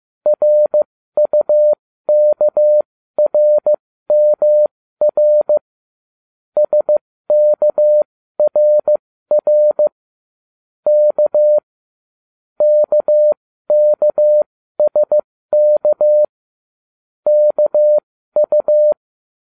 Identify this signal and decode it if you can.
Morse code, heard as a keyed tone, about 15 wpm, spaced as for 13 wpm, 600 Hz. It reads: RUKRMR SKRR K KKSK KU